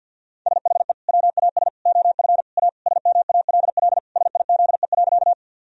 Morse code, heard as a keyed tone, about 50 wpm, 700 Hz. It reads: SHE WAS OF A SMALL SIZE.
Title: Sentences from Top 1000 Words - Review Part 2 - 50wpm